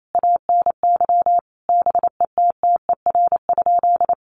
Morse code, heard as a keyed tone, about 28 wpm, 700 Hz. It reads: ANY BETTER?
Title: Sentences from Top 1200 Words 28wpm